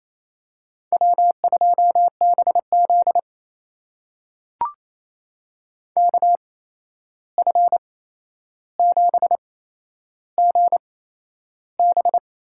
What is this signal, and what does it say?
Morse code, heard as a keyed tone, about 28 wpm, spaced as for 11 wpm, 700 Hz. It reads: W2BZ KF7GB
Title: USA Call Signs - 8x Character Spacing - 28wpm